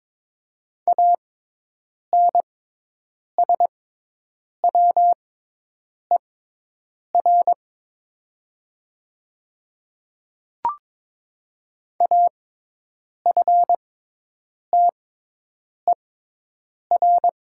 Morse code, heard as a keyed tone, about 22 wpm, spaced as for 8 wpm, 700 Hz. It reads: ANSWER AFTER
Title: 6x Character Spacing - Top 500 Words - 22wpm